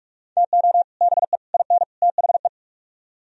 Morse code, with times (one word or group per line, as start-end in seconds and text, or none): 0.4-0.8 TO
1.0-1.4 BE
1.5-1.8 IN
2.0-2.5 THE